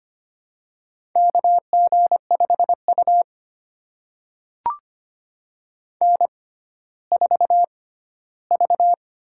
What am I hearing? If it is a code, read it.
Morse code, heard as a keyed tone, about 25 wpm, spaced as for 14 wpm, 700 Hz. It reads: KG5U N4V
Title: USA Call Signs - 4 Characters - 6x Character Spacing - 25wpm